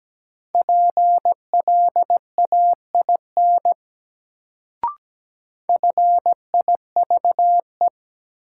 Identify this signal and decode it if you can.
Morse code, heard as a keyed tone, about 17 wpm, 700 Hz. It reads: PLAIN FIVE
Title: Sentences from Top 400 Words - New Word Drill - 17wpm